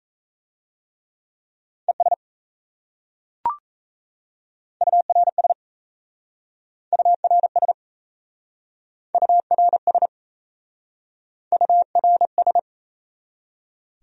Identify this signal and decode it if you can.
Morse code, heard as a keyed tone, about 35 wpm, 700 Hz. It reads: ES URS URS URS URS